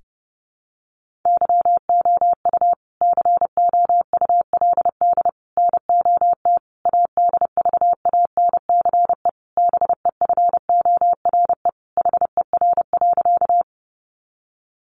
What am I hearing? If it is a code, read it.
Morse code, heard as a keyed tone, about 30 wpm, 700 Hz. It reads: YOU COULD NOT ADVANCE BEFORE HER.
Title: Sentences from Top 1400 Words 30wpm